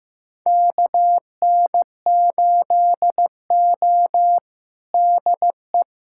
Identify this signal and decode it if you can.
Morse code, heard as a keyed tone, about 15 wpm, 700 Hz. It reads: KN8O DE
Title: USA Callsign DE Callsign - 4 Characters - 15wpm